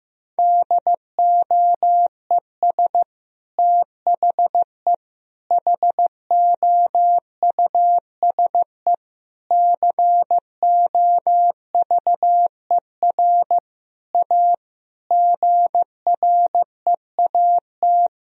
Morse code, heard as a keyed tone, about 15 wpm, 700 Hz. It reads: DOES THE HOUSE COVER A GREAT
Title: Sentences from Top 1000 Words - Review Part 2 - 15wpm